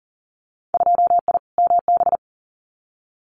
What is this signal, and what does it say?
Morse code, heard as a keyed tone, about 42 wpm, 700 Hz. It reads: 2I MB
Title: Winter Field Day Exchanges Encore 40wpm